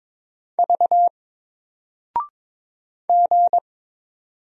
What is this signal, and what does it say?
Morse code, heard as a keyed tone, about 23 wpm, 700 Hz. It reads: V G
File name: Single-Letter-Number-22wpm